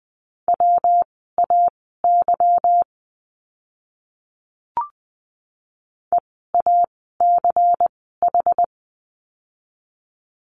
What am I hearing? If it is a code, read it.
Morse code, heard as a keyed tone, about 20 wpm, spaced as for 14 wpm, 700 Hz. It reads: WAY EACH
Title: 192 Lesson 31f - Letter v - Words 20wpm